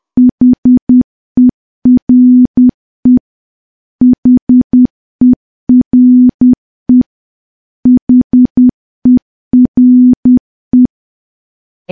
Morse code, heard as a keyed tone, about 10 wpm, 250 Hz. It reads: HERE HERE HERE
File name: morse / morse-training-10WPM